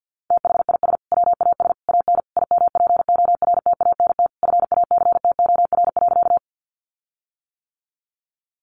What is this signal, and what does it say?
Morse code, heard as a keyed tone, about 50 wpm, 700 Hz. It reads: THIS WAS AN IMPORTANT FACTOR.
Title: Sentences from Top 1300 Words 50wpm